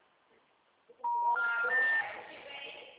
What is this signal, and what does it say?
Special information tone, with speech.